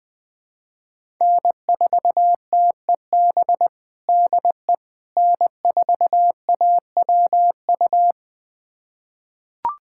Morse code, heard as a keyed tone, about 21 wpm, 700 Hz. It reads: N4TEB DE N4AWU
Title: USA Callsign DE Callsign - 5 Characters - 20wpm